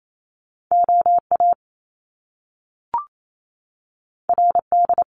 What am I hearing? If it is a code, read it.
Morse code, heard as a keyed tone, about 28 wpm, 700 Hz. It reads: OA RD
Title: Top Two Letter Combinations in Words 28wpm